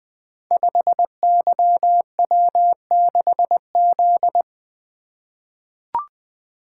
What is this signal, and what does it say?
Morse code, heard as a keyed tone, about 20 wpm, 700 Hz. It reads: 5YW6Z
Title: Code Groups - 5 Letter:Numbers - 20wpm